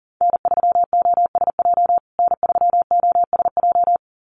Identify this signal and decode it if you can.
Morse code, heard as a keyed tone, about 40 wpm, 700 Hz. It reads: N3OSJ N3OSJ
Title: USA Call Signs Repeated 2x 40wpm